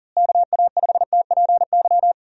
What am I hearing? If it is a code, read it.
Morse code, heard as a keyed tone, about 38 wpm, 700 Hz. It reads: KA5TPY
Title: USA Call Signs - 6 Characters - Repeated 3x - 40wpm